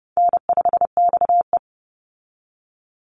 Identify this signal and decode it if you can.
Morse code, heard as a keyed tone, about 30 wpm, 700 Hz. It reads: N5XE
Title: Call Signs - 30wpm